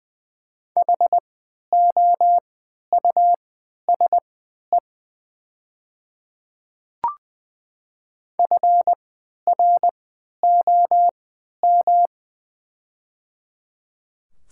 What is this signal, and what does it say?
Morse code, heard as a keyed tone, about 20 wpm, spaced as for 11 wpm, 700 Hz. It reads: HOUSE FROM
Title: 3x Character Spacing - Top 200 Words - 20wpm